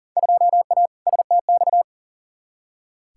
Morse code, heard as a keyed tone, about 38 wpm, 700 Hz. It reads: 2A STX
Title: Field Day Exchanges - 40wpm